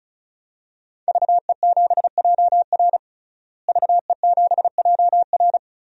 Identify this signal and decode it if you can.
Morse code, heard as a keyed tone, about 36 wpm, 700 Hz. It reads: VE7JR VE7JR
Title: DXCC-100 Call Signs - 5 Characters - Repeated 2x - 35wpm